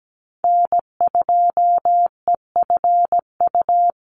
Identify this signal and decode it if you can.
Morse code, heard as a keyed tone, about 17 wpm, 700 Hz. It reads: N2EFU
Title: USA Call Signs 17wpm